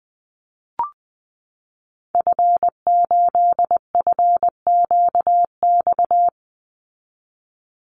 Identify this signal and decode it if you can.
Morse code, heard as a keyed tone, about 20 wpm, 700 Hz. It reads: F8FQX
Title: Call Signs Repeated 3x - 20wpm